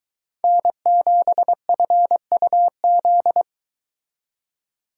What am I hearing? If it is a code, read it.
Morse code, heard as a keyed tone, about 22 wpm, 700 Hz. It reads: N7FUZ